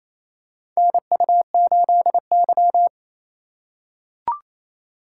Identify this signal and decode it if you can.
Morse code, heard as a keyed tone, about 28 wpm, 700 Hz. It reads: NU8Y
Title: USA Call Signs - 4x Character Spacing - 28wpm